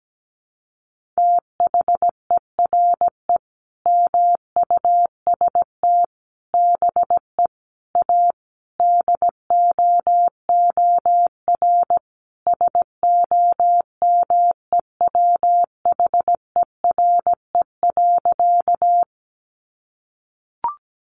Morse code, heard as a keyed tone, about 17 wpm, 700 Hz. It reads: THERE MUST BE A DOOR SOMEWHERE.
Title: Sentences from Top 1400 Words 17wpm